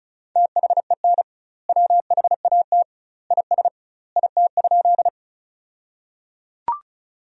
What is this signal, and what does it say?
Morse code, heard as a keyed tone, about 34 wpm, 700 Hz. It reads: THEN WHAT IS IT?
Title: 2x Word-Spacing - Sentences from Top 100 Words - 35wpm